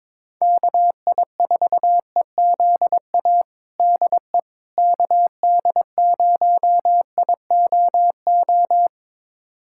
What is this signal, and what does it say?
Morse code, heard as a keyed tone, about 22 wpm, 700 Hz. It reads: KI4EZA DE KD0IOO
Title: USA Callsign DE Callsign - 6 Characters - 22wpm